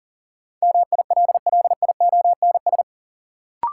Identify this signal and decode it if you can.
Morse code, heard as a keyed tone, about 40 wpm, 700 Hz. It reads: MILLIONS
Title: Sets of 5 Words - New Word Drill - 40wpm